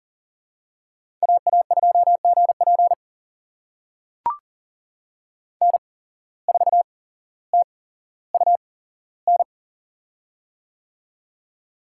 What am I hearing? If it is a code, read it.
Morse code, heard as a keyed tone, about 40 wpm, spaced as for 17 wpm, 700 Hz. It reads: AA2GP N4TUN
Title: USA Call Signs - 5 Characters - 8x Character Spacing - 40wpm